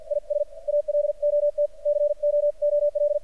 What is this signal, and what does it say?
Morse code, heard as a keyed tone, about 49 wpm, 590 Hz. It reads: IS ALOT COOL
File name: archived / CRSnet